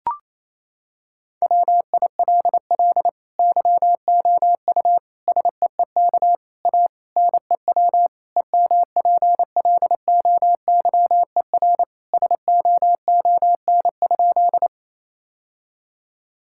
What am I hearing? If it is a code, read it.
Morse code, heard as a keyed tone, about 28 wpm, 700 Hz. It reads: WILL YOU SEEK A NEW EMPLOYER SOON?
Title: Sentences from Top 1600 Words 28wpm